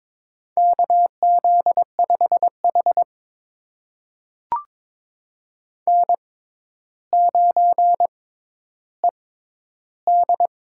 Morse code, heard as a keyed tone, about 21 wpm, spaced as for 11 wpm, 700 Hz. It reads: KZ5H N9ED
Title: USA Call Signs - 4 Characters - 6x Character Spacing - 22wpm